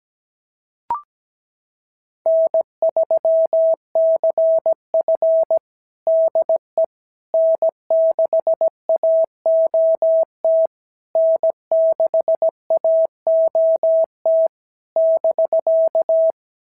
Morse code, heard as a keyed tone, about 17 wpm, 650 Hz. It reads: N3CF DE N6AOT N6AOT <BK>
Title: SOTA QSO Practice 17wpm